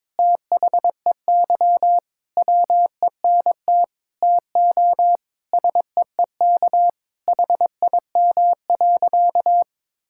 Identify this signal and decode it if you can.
Morse code, heard as a keyed tone, about 22 wpm, 700 Hz. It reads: THEY WENT TO SEEK HIM.
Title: Sentences from Top 1600 Words 22wpm